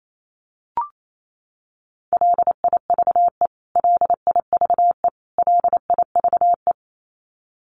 Morse code, heard as a keyed tone, about 28 wpm, 700 Hz. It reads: LIVE LIVE LIVE
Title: Top 500 Words Repeated 28wpm